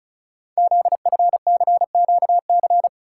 Morse code, heard as a keyed tone, about 36 wpm, 700 Hz. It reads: ZFCQC
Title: Code Groups - 5 Letters - 35wpm